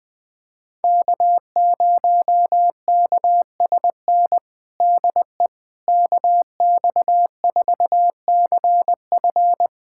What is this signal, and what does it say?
Morse code, heard as a keyed tone, about 20 wpm, 700 Hz. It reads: K0KSN DE KX4CF
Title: USA Callsign DE Callsign 20wpm